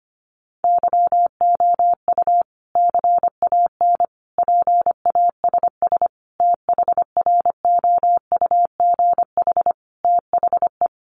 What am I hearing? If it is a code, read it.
Morse code, heard as a keyed tone, about 25 wpm, 700 Hz. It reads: YOU CAN PASS THROUGH THE
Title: Sentences from Top 400 Words 25wpm